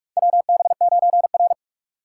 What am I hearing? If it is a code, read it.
Morse code, heard as a keyed tone, about 47 wpm, 700 Hz. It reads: WB9R
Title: USA Call Signs - 4 Characters - 3x Character Spacing - 45wpm